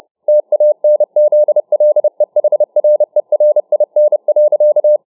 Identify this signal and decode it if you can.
Morse code, heard as a keyed tone, about 30 wpm, 600 Hz. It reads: TANZLEHRERIN.